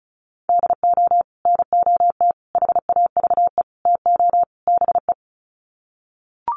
Morse code, heard as a keyed tone, about 34 wpm, 700 Hz. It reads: DO NOT HAVE TO BE
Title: Sets of 5 Words 35wpm